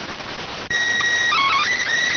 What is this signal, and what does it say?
SSTV (slow-scan television), PD 290.